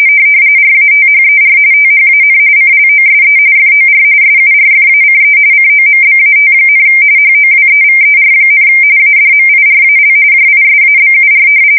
RTTY, about 45.45 baud. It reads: RATHER THAN CLOSE FOR SURFACE COMBAT, HOWEVER, ADMIRAL JISABUR
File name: fsk170 / battleshipnj